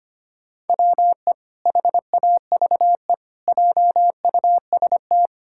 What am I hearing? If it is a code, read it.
Morse code, heard as a keyed tone, about 25 wpm, 700 Hz. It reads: WE HAVE JUST
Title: Pangram Challenge v2 - 25wpm